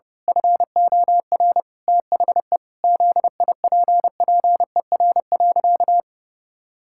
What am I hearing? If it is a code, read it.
Morse code, heard as a keyed tone, about 30 wpm, 700 Hz. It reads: FOR THE ZIPPER.
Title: Pangram Challenge v2 - 30wpm